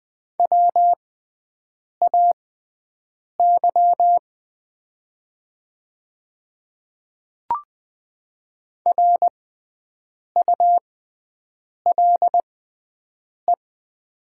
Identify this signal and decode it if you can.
Morse code, heard as a keyed tone, about 20 wpm, spaced as for 7 wpm, 700 Hz. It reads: WAY RULE
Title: 6x Character Spacing - Top 500 Words - 20wpm